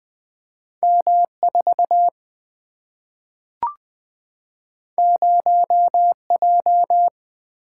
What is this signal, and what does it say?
Morse code, heard as a keyed tone, about 20 wpm, 700 Hz. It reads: M4 0J
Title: Single-Letter Single-Number 20wpm